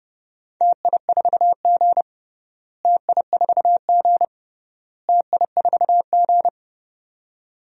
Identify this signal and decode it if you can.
Morse code, heard as a keyed tone, about 29 wpm, 700 Hz. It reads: TI4G TI4G TI4G
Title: DXCC-100 Call Signs - 4 Characters - Repeated 3x - 30wpm